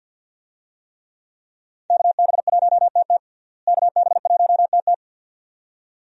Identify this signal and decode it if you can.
Morse code, heard as a keyed tone, about 46 wpm, 700 Hz. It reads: KB1TT KB1TT
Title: USA Call Signs - 5 Characters - Repeated 2x - 50wpm